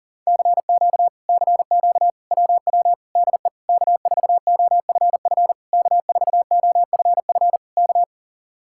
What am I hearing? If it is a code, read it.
Morse code, heard as a keyed tone, about 43 wpm, 700 Hz. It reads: CQ CQ WW DE K4OFF K4OFF K